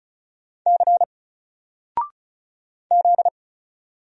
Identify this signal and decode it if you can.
Morse code, heard as a keyed tone, about 36 wpm, 700 Hz. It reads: C Z
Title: Single Character (Letter:Number:Punctuation:Prosign) 35wpm